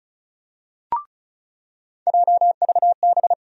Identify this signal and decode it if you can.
Morse code, heard as a keyed tone, about 36 wpm, 700 Hz. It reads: JVB
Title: USA Call Sign Postfixes - 3 Letters 35wpm